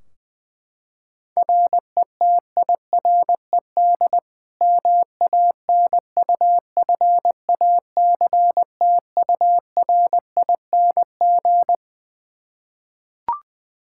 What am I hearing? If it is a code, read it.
Morse code, heard as a keyed tone, about 20 wpm, 700 Hz. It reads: RETIRED MANUFACTURING
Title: K1USN SST QSO Practice 20wpm